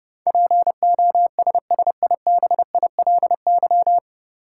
Morse code, heard as a keyed tone, about 32 wpm, 700 Hz. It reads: POSSIBILY
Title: Sentences from Top 1300 Words - New Word Drill Alt - 30wpm